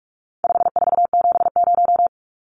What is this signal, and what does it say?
Morse code, heard as a keyed tone, about 44 wpm, 700 Hz. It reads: 5470
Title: Numbers - 4-Digit 45wpm